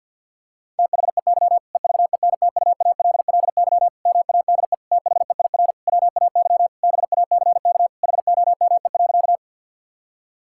Morse code, heard as a keyed tone, about 52 wpm, 700 Hz. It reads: THEY EVENTUALLY MADE THEIR WAY BACK HOME.
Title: Sentences from Top 1000 Words - Review Part 6 - 50wpm